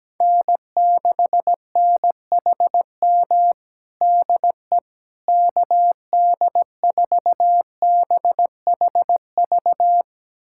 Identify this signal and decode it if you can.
Morse code, heard as a keyed tone, about 17 wpm, 700 Hz. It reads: N6NHM DE KD4BHV
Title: USA Callsign DE Callsign 17wpm